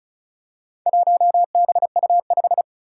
Morse code, heard as a keyed tone, about 36 wpm, 700 Hz. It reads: 1BU5
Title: Four Challenging Letter-Number Combinations 35wpm